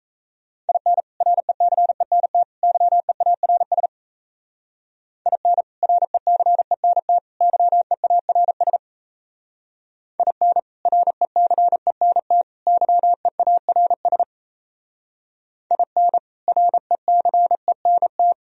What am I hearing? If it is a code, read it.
Morse code, heard as a keyed tone, about 34 wpm, 700 Hz. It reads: IN RECENT YEARS IN RECENT YEARS IN RECENT YEARS IN RECENT
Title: Speed-Racing - Sets of 3 Words 28wpm